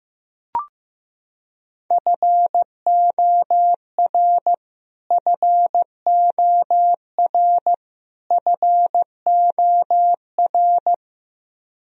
Morse code, heard as a keyed tone, about 16 wpm, 700 Hz. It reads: FOR FOR FOR
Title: Top 100 Words Repeated 15wpm